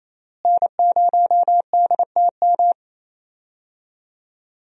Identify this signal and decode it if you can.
Morse code, heard as a keyed tone, about 28 wpm, 700 Hz. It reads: N0DTM
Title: USA Call Signs - 5 Characters - Repeated 3x - 28wpm